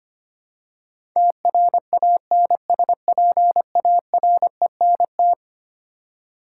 Morse code, heard as a keyed tone, about 25 wpm, 700 Hz. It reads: TRANSPARENT
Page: morse-code-ninja-practice-25wpm